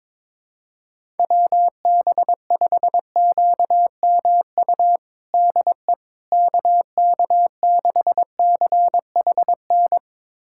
Morse code, heard as a keyed tone, about 22 wpm, 700 Hz. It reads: WB5QMU DE KK6CHN
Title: USA Callsign DE Callsign 22wpm